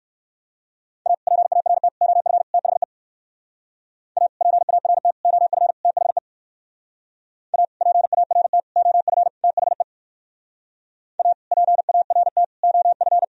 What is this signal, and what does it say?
Morse code, heard as a keyed tone, about 57 wpm, 700 Hz. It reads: A PART OF THE A PART OF THE A PART OF THE A PART OF